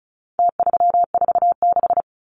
Morse code, heard as a keyed tone, about 34 wpm, 700 Hz. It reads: T346
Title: Numbers - Contest Style - 4-Digit 35wpm